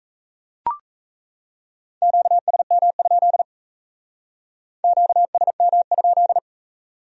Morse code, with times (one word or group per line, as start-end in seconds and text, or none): 2.0-3.4 QSM?
4.8-6.4 QSM?